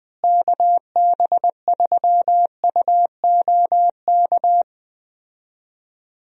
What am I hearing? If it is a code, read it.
Morse code, heard as a keyed tone, about 21 wpm, 700 Hz. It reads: KB3UOK
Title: USA Call Signs - 6 Characters - Repeated 3x - 20wpm